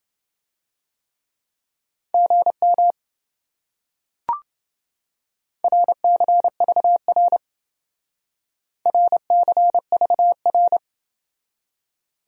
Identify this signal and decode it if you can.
Morse code, heard as a keyed tone, about 28 wpm, 700 Hz. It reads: GM RCVR RCVR